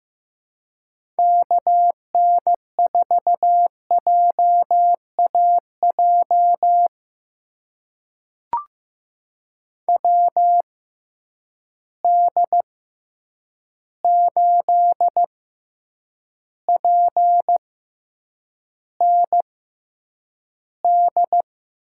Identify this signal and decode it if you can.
Morse code, heard as a keyed tone, about 15 wpm, spaced as for 8 wpm, 700 Hz. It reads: KN4JAJ WD8PND
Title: USA Call Signs - 6 Characters - 6x Character Spacing - 15wpm